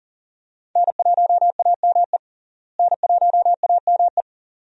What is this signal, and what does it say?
Morse code, heard as a keyed tone, about 40 wpm, 700 Hz. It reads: N1AME N1AME
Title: USA Call Signs - 5 Characters - Repeated 2x - 40wpm